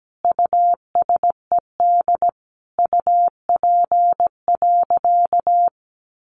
Morse code, heard as a keyed tone, about 17 wpm, 700 Hz. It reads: USED UP.
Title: Sentences from Top 900 Words 17wpm